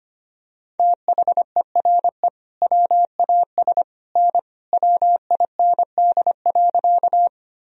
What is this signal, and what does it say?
Morse code, heard as a keyed tone, about 25 wpm, 700 Hz. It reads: THERE WAS N WIND.
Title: Sentences from Top 400 Words and Abbreviations 25wpm